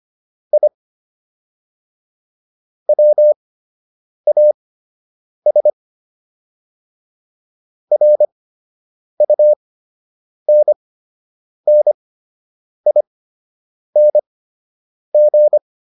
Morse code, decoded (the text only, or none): I WAS RUNNING